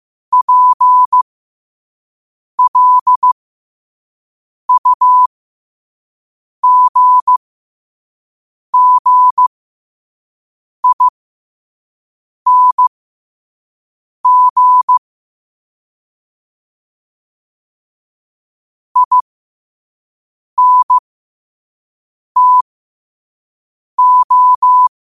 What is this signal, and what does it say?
Morse code, heard as a keyed tone, about 15 wpm, spaced as for 5 wpm, 1000 Hz. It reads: PLUGGING INTO
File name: qotdfiles / 5wpm